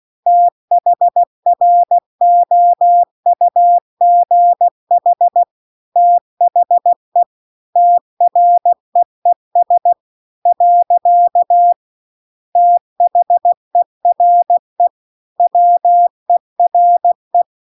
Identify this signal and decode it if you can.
Morse code, heard as a keyed tone, about 16 wpm, 700 Hz. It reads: THROUGH THE TREES . THERE WERE
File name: CW-The-Big-Twoheaded-River-2-16WPM